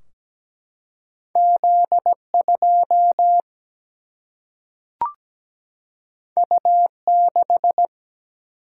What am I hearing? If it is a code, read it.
Morse code, heard as a keyed tone, about 17 wpm, 700 Hz. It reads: Z2 U6
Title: Single-Letter Single-Number 17wpm